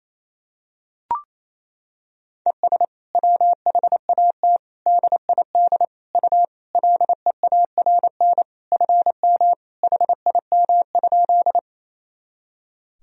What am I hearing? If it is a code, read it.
Morse code, heard as a keyed tone, about 28 wpm, 700 Hz. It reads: ES WHAT DID U LEARN FM HIM?
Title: Sentences from Top 300 Words and Abbreviations 28wpm